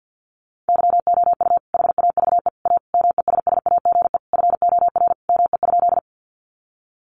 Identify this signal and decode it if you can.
Morse code, heard as a keyed tone, about 50 wpm, 700 Hz. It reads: YOU HAVE A MESSAGE FOR ME?
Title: Sentences from Top 1100 Words 50wpm